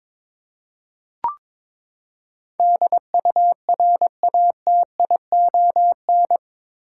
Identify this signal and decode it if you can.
Morse code, heard as a keyed tone, about 22 wpm, 700 Hz. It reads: DURATION
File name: practice / 22wpm / Eight Letter Words - 22wpm